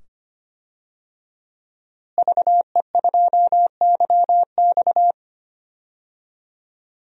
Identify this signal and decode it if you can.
Morse code, heard as a keyed tone, about 25 wpm, 700 Hz. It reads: VE2YX
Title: DXCC-100 Call Signs - 5 Characters - Repeated 2x - 25wpm